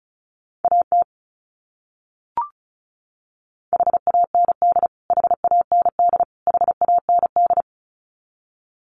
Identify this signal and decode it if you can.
Morse code, heard as a keyed tone, about 34 wpm, 700 Hz. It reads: AT HAND HAND HAND
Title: Top 300 Words Repeated 35wpm